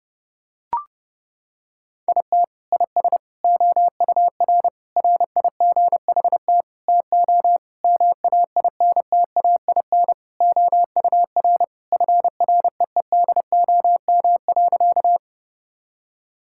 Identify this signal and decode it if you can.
Morse code, heard as a keyed tone, about 30 wpm, 700 Hz. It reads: IT IS OUR RIGHT TO MAINTAIN OUR FREEDOM.
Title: Sentences from Top 1000 Words - Review Part 5 - 30wpm